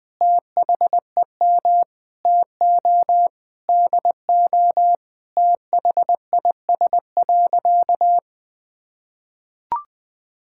Morse code, heard as a keyed tone, about 21 wpm, 700 Hz. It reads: THEM TO DO THIS.
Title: Sentences from Top 1000 Words - Review Part 5 - 20wpm